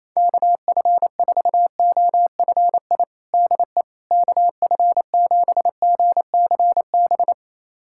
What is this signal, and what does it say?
Morse code, heard as a keyed tone, about 28 wpm, 700 Hz. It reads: KF4OFI DE KF7GCB
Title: USA Callsign DE Callsign - 6 Characters - 28wpm